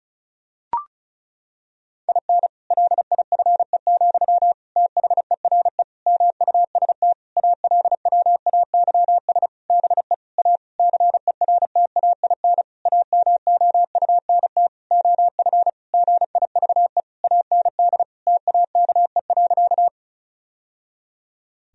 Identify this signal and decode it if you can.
Morse code, heard as a keyed tone, about 36 wpm, 700 Hz. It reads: IN LIFE, THERE MUST ALWAYS BE A CERTAIN AMOUNT OF GIVE AND TAKE.
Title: Sentences from Top 1000 Words - Review Part 4 - 35wpm